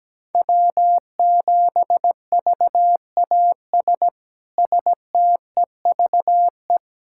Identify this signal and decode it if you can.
Morse code, heard as a keyed tone, about 17 wpm, 700 Hz. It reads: W7VAS STEVE
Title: K1USN SST QSO Practice - 17wpm